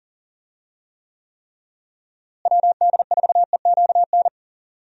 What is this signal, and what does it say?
Morse code, heard as a keyed tone, about 40 wpm, 700 Hz. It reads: WD4EQN
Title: USA Call Signs - 6 Characters - Repeated 3x - 40wpm